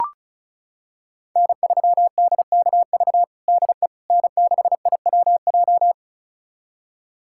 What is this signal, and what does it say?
Morse code, heard as a keyed tone, about 36 wpm, 700 Hz. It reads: N3DKV DE N6IWJ
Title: USA Callsign DE Callsign - 5 Characters - 35wpm